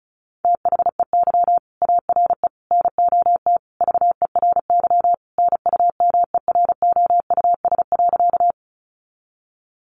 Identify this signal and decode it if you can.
Morse code, heard as a keyed tone, about 34 wpm, 700 Hz. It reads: THEY ARE NOT VERY NUMEROUS.
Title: Sentences from Top 1400 Words 35wpm